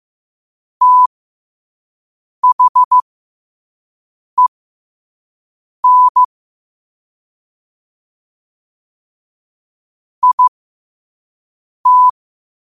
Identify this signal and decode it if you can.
Morse code, heard as a keyed tone, about 15 wpm, spaced as for 5 wpm, 1000 Hz. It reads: THEN IT
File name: qotdfiles / 5wpm